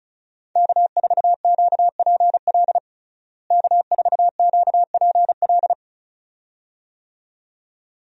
Morse code, heard as a keyed tone, about 34 wpm, 700 Hz. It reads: K4QPL K4QPL
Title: USA Call Signs - 5 Characters - Repeated 2x - 35wpm